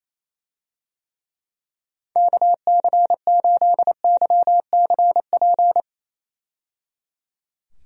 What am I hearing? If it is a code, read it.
Morse code, heard as a keyed tone, about 28 wpm, 700 Hz. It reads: KC8YCP
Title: USA Call Signs - 6 Characters - 28wpm